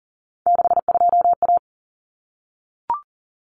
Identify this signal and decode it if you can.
Morse code, heard as a keyed tone, about 40 wpm, 700 Hz. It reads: 62A
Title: Common Cut Numbers - 3-Digit 40wpm